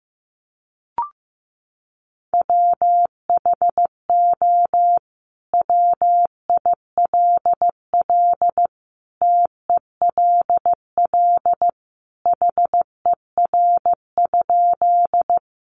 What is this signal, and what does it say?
Morse code, heard as a keyed tone, about 14 wpm, 700 Hz. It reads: WHO WILL TELL HER?
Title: Sentences from Top 200 Words 15wpm